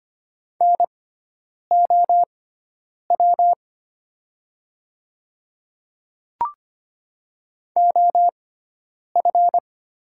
Morse code, heard as a keyed tone, about 25 wpm, spaced as for 9 wpm, 700 Hz. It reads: NOW OF
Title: 6x Character Spacing - Top 300 Words - 25wpm